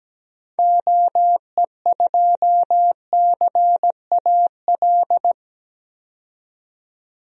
Morse code, heard as a keyed tone, about 17 wpm, 700 Hz. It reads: OE2CAL
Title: DXCC-100 Call Signs - 6 Characters - 17wpm